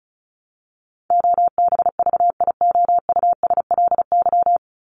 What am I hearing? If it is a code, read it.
Morse code, heard as a keyed tone, about 34 wpm, 700 Hz. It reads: OBVIOUSLY
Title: Top 1501-1750 Words 35wpm